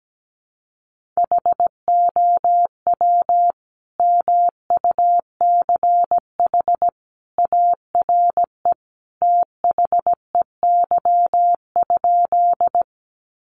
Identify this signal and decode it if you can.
Morse code, heard as a keyed tone, about 17 wpm, 700 Hz. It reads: HOW MUCH ARE THEY?
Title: Sentences from Top 200 Words 17wpm